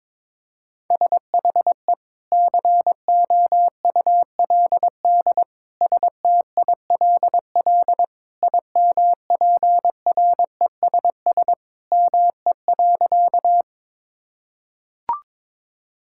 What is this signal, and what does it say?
Morse code, heard as a keyed tone, about 22 wpm, 700 Hz. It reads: SHE COULD STILL IMPRESS ME.
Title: Sentences from Top 1600 Words 22wpm